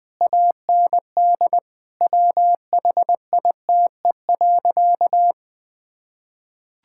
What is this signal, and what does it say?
Morse code, heard as a keyed tone, about 20 wpm, 700 Hz. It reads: AND WHITE.
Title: Sentences with 5-Word Phrases - 20wpm